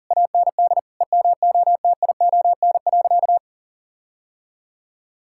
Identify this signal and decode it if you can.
Morse code, heard as a keyed tone, about 40 wpm, 700 Hz. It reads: AND EMOTION.
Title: Sentences from Top 1600 Words 40wpm